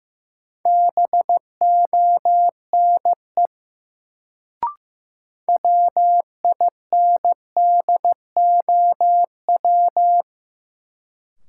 Morse code, heard as a keyed tone, about 15 wpm, 700 Hz. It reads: BONE WINDOW
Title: Sentences from Top 1600 Words - New Word Drill - 15wpm